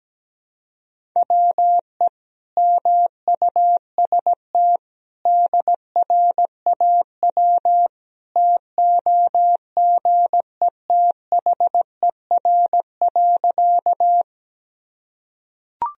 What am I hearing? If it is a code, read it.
Morse code, heard as a keyed tone, about 17 wpm, 700 Hz. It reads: WE MUST DRAW TOGETHER.
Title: Sentences from Top 1000 Words - Review Part 1 - 17wpm